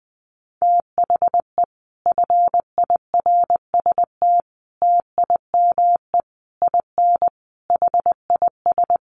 Morse code, heard as a keyed tone, about 20 wpm, 700 Hz. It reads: THE FIRST TIME IN HIS
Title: Sets of 5 Words Encore2 20wpm